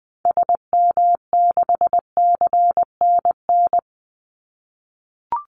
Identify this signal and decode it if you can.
Morse code, heard as a keyed tone, about 20 wpm, 700 Hz. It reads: SM6CNN
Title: Call Signs - 20wpm